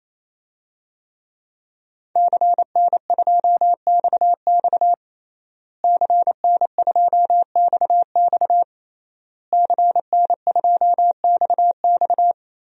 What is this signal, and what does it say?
Morse code, heard as a keyed tone, about 28 wpm, 700 Hz. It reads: CN2XX CN2XX CN2XX
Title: DXCC-100 Call Signs - 5 Characters - Repeated 3x - 28wpm